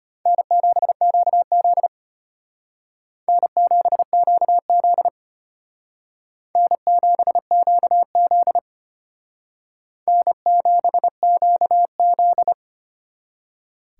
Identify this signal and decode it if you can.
Morse code, heard as a keyed tone, about 29 wpm, 700 Hz. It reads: N7QZ N7QZ N7QZ N7QZ
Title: Speed Racing - USA Call Signs 25wpm